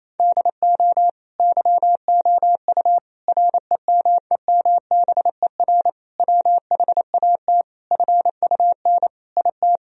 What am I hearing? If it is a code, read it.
Morse code, heard as a keyed tone, about 28 wpm, 700 Hz. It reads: DO YOU REMEMBER WHAT FUN IT
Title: Sentences from Top 1000 Words - Review Part 4 - 28wpm